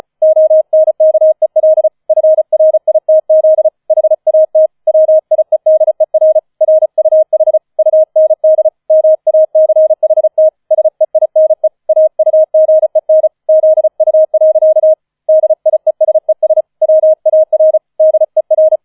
Morse code, heard as a keyed tone, about 34 wpm, 630 Hz. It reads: ONKEL FRITZ HAT WIEDER RUH UND MACHT SEINE AUGEN ZU. DIESES WAR DER
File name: 005 Fünfter Streich 35wpm